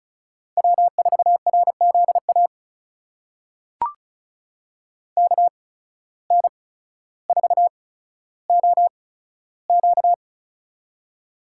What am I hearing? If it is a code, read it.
Morse code, heard as a keyed tone, about 36 wpm, spaced as for 15 wpm, 700 Hz. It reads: W4RZA KN4OQ
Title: USA Call Signs - 5 Characters - 8x Character Spacing - 35wpm